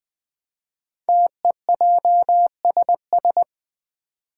Morse code, heard as a keyed tone, about 20 wpm, 700 Hz. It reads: TEJSS
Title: Code Groups - 5 Letters - 20wpm